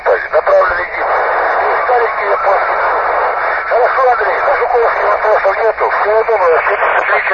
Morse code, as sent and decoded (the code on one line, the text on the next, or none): none